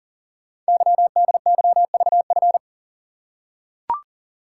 Morse code, heard as a keyed tone, about 40 wpm, 700 Hz. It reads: YDYVF
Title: Code Groups - 5 Letters - 40wpm